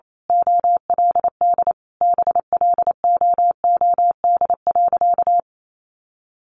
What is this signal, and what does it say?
Morse code, heard as a keyed tone, about 28 wpm, 700 Hz. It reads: OLD BLOOD.
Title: A Sound Of Thunder 28wpm